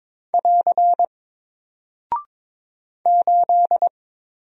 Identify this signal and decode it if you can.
Morse code, heard as a keyed tone, about 22 wpm, 700 Hz. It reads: <AR> 8
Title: Single Character (Letter:Number:Punctuation:Prosign) Alt 22wpm